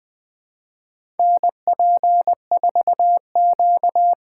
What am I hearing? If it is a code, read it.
Morse code, heard as a keyed tone, about 21 wpm, 700 Hz. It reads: NP4Q
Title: USA Call Signs - 5 Characters - 6x Character Spacing - 20wpm